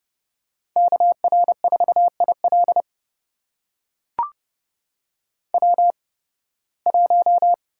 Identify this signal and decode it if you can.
Morse code, heard as a keyed tone, about 30 wpm, spaced as for 20 wpm, 700 Hz. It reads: KR4IL W1
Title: USA Call Signs - 8x Character Spacing - 30wpm